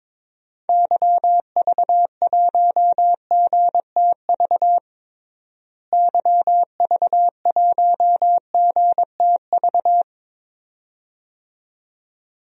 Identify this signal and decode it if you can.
Morse code, heard as a keyed tone, about 21 wpm, 700 Hz. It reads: YV1GTV YV1GTV
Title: DXCC-100 Call Signs - 6 Characters - Repeated 2x - 22wpm